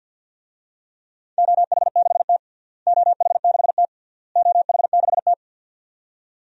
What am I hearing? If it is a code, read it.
Morse code, heard as a keyed tone, about 50 wpm, 700 Hz. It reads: OH6T OH6T OH6T
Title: DXCC-100 Call Signs - 4 Characters - Repeated 3x - 50wpm